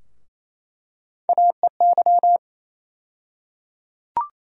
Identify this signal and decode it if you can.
Morse code, heard as a keyed tone, about 28 wpm, 700 Hz. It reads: AEY
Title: USA Call Sign Postfixes - 3 Letters 28wpm